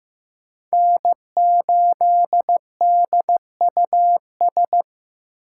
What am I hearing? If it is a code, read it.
Morse code, heard as a keyed tone, about 15 wpm, 700 Hz. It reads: N8DUS
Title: K1USN SST QSO Practice - 15wpm